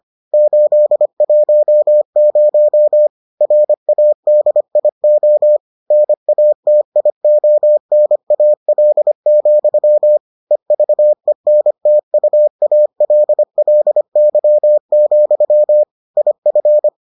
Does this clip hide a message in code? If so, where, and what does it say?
Yes: Morse code from 0.3 to 16.9 s, heard as a keyed tone, about 25 wpm, 600 Hz. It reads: 810 RADIO NATIONAL, EVENTUALLY, IF